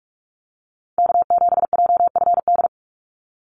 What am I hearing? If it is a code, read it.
Morse code, heard as a keyed tone, about 44 wpm, 700 Hz. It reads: K7JFD
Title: NA SOTA Chaser Callsigns 45wpm